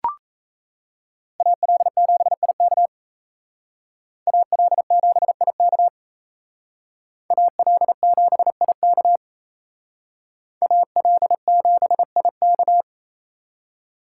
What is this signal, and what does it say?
Morse code, heard as a keyed tone, about 34 wpm, 700 Hz. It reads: AL7IK AL7IK AL7IK AL7IK
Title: Speed Racing - USA 2x2 Call Signs 28wpm